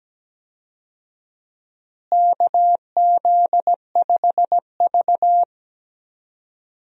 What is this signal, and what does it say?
Morse code, heard as a keyed tone, about 17 wpm, 700 Hz. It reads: KZ5V
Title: USA Call Signs - 4 Characters - Repeated 3x - 17wpm